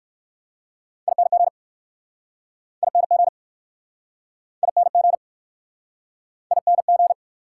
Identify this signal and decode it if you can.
Morse code, heard as a keyed tone, about 60 wpm, 690 Hz. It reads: ING ING ING ING